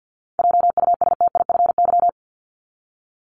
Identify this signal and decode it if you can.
Morse code, heard as a keyed tone, about 50 wpm, 700 Hz. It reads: JUSTIFY
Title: Top 1751-2000 Words 50wpm